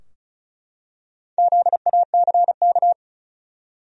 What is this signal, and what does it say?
Morse code, heard as a keyed tone, about 34 wpm, 700 Hz. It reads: ZACK
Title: USA Common Names - Max 5 Letters - 35wpm